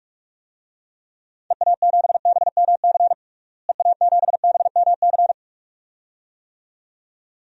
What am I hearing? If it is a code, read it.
Morse code, heard as a keyed tone, about 46 wpm, 690 Hz. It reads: EA7BMC EA7BMC
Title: DXCC-100 Call Signs - 6 Characters - Repeated 2x - 45wpm